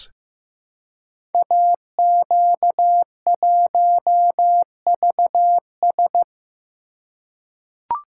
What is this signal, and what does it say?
Morse code, heard as a keyed tone, about 15 wpm, 700 Hz. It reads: AQ1VS
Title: Code Groups - 5 Letter:Numbers - 15wpm